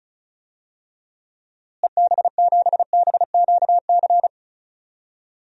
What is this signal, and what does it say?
Morse code, heard as a keyed tone, about 36 wpm, 700 Hz. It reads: EB7BQC
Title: DXCC-100 Call Signs - 6 Characters - Repeated 2x - 35wpm